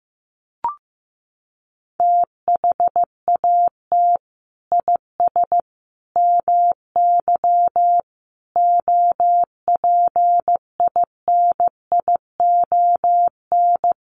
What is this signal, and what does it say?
Morse code, heard as a keyed tone, about 15 wpm, 700 Hz. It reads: THAT IS MY OPINION
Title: Sentences from Top 1200 Words 15wpm